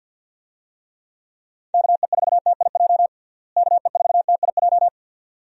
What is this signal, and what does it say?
Morse code, heard as a keyed tone, about 50 wpm, 700 Hz. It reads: KE4TIJ KE4TIJ
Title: USA Call Signs - 6 Characters - Repeated 2x - 50wpm